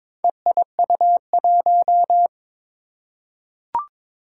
Morse code, heard as a keyed tone, about 22 wpm, 700 Hz. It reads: EIU1
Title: Four Challenging Letter-Number Combinations 22wpm